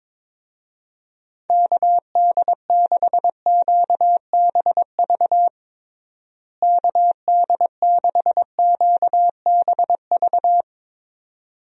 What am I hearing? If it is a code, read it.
Morse code, heard as a keyed tone, about 22 wpm, 700 Hz. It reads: KD6QBV KD6QBV
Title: USA Call Signs - 6 Characters - Repeated 2x - 22wpm